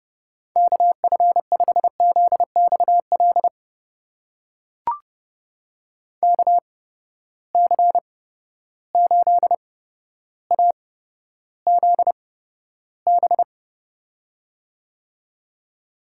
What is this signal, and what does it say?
Morse code, heard as a keyed tone, about 30 wpm, spaced as for 13 wpm, 700 Hz. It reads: KF5ZXL KC8AZB
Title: USA Call Signs - 6 Characters - 8x Character Spacing - 30wpm